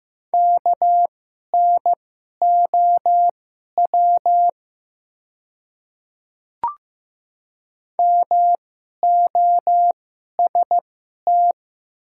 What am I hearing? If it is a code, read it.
Morse code, heard as a keyed tone, about 15 wpm, spaced as for 11 wpm, 700 Hz. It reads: KNOW MOST
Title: 2x Character Spacing - Top 100 Words - 15wpm